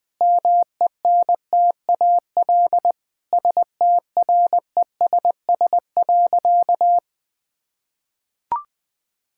Morse code, heard as a keyed tone, about 20 wpm, 700 Hz. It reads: MENTAL STRESS.
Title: Sentences from Top 1000 Words - Review Part 6 - 20wpm